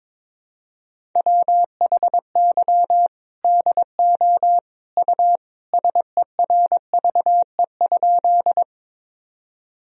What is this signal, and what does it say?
Morse code, heard as a keyed tone, about 22 wpm, 700 Hz. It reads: WHY DO U SERVE?
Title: Sentences from Top 400 Words and Abbreviations 22wpm